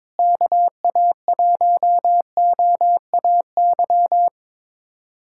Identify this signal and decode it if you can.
Morse code, heard as a keyed tone, about 22 wpm, 700 Hz. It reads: KA1OAY
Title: USA Callsign DE Callsign 22wpm